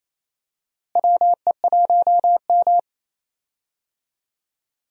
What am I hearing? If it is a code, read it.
Morse code, heard as a keyed tone, about 28 wpm, 700 Hz. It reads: WE1M
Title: USA Call Signs - 4 Characters - Repeated 3x - 28wpm